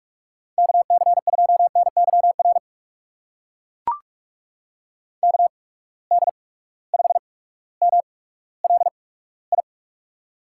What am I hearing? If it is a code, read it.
Morse code, heard as a keyed tone, about 42 wpm, spaced as for 19 wpm, 700 Hz. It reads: KC2NYR KD5MLI
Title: USA Call Signs - 6 Characters - 8x Character Spacing - 45wpm